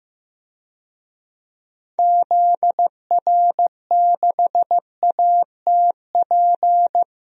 Morse code, heard as a keyed tone, about 15 wpm, 700 Hz. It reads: ZR6ATP
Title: DXCC-100 Call Signs - 6 Characters - 15wpm